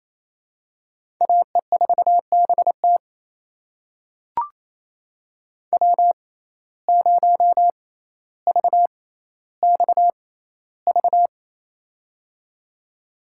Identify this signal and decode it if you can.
Morse code, heard as a keyed tone, about 28 wpm, spaced as for 14 wpm, 700 Hz. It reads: AE4BT W0VXV
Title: USA Call Signs - 5 Characters - 6x Character Spacing - 28wpm